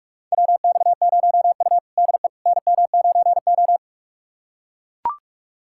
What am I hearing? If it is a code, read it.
Morse code, heard as a keyed tone, about 41 wpm, 700 Hz. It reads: WX0U DE NM9O
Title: USA Callsign DE Callsign - 4 Characters - 45wpm